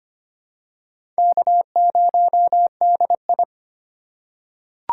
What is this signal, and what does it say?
Morse code, heard as a keyed tone, about 25 wpm, 700 Hz. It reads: K0DI
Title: USA Call Signs - 4 Characters - 4x Character Spacing - 25wpm